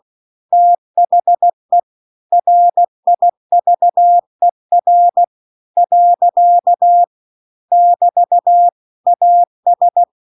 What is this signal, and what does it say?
Morse code, heard as a keyed tone, about 16 wpm, 700 Hz. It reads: THE RIVER . <BT> AS